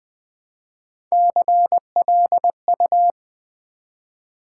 Morse code, heard as a keyed tone, about 20 wpm, 700 Hz. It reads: CLU
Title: Build A Word - 1 Letter At A Time - 3 to 5-Letter Words - 20wpm